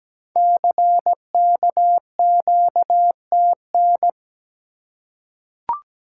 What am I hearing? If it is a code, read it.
Morse code, heard as a keyed tone, about 17 wpm, 700 Hz. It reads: CKQTN
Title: Code Groups - 5 Letters - 17wpm